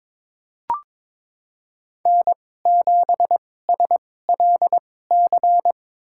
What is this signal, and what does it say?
Morse code, heard as a keyed tone, about 22 wpm, spaced as for 16 wpm, 700 Hz. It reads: N7SLC
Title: USA Call Signs - 5 Characters - 2x Character Spacing - 22wpm